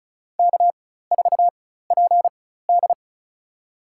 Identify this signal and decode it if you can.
Morse code, heard as a keyed tone, about 34 wpm, spaced as for 16 wpm, 700 Hz. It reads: K4PD
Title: USA Call Signs - 4 Characters - 4x Character Spacing - 35wpm